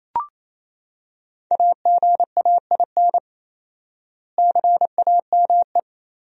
Morse code, heard as a keyed tone, about 28 wpm, 700 Hz. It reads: AGAIN CAME
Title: 4x Word-Spacing - Sentences from Top 500 Words - 28wpm